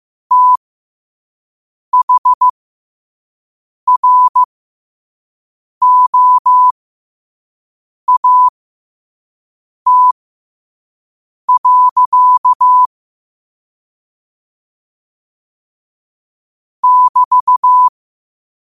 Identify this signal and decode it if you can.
Morse code, heard as a keyed tone, about 15 wpm, spaced as for 5 wpm, 1000 Hz. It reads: THROAT. <BT>
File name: qotdfiles / 5wpm